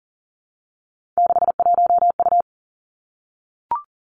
Morse code, heard as a keyed tone, about 42 wpm, 700 Hz. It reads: 61U
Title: Three Challenging Letter-Number Combinations 40wpm